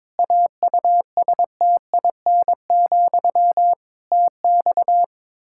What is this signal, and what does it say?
Morse code, heard as a keyed tone, about 22 wpm, 700 Hz. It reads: AUSTIN, TX
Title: Practice QSO - Exchange 1 - Beginning - 22wpm